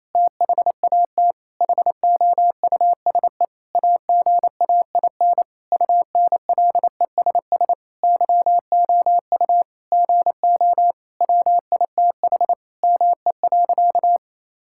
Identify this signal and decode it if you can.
Morse code, heard as a keyed tone, about 28 wpm, 700 Hz. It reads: THAT HOUSE AGAIN UNLESS YOU GO WITH ME.
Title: Sentences from Top 1000 Words - Review Part 6 - 28wpm